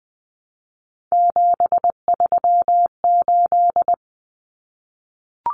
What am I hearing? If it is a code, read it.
Morse code, heard as a keyed tone, about 20 wpm, 700 Hz. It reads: 738
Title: Numbers - 3-Digit 20wpm